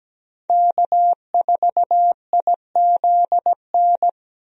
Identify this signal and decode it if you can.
Morse code, heard as a keyed tone, about 17 wpm, 700 Hz. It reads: K4IZN
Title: USA Call Signs - 5 Characters - Repeated 2x - 17wpm